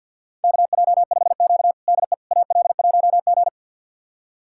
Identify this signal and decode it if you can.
Morse code, heard as a keyed tone, about 51 wpm, 700 Hz. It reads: KJ5Q DE AL1G
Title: USA Callsign DE Callsign - 4 Characters - 50wpm